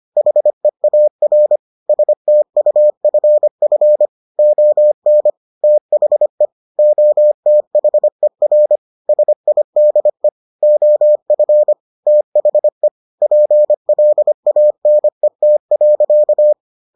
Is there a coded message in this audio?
Yes: Morse code from 0.2 to 16.5 s, heard as a keyed tone, about 25 wpm, 600 Hz. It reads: HEAR STUFF ON THE OTHER SIDE OF THE PLANET.